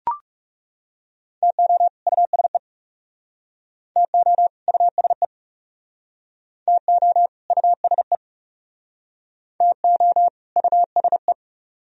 Morse code, heard as a keyed tone, about 38 wpm, 700 Hz. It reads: TO USE TO USE TO USE TO USE